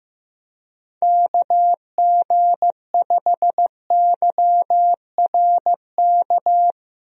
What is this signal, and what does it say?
Morse code, heard as a keyed tone, about 15 wpm, 700 Hz. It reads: KG5YRK
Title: USA Call Signs - 6 Characters - Repeated 3x - 15wpm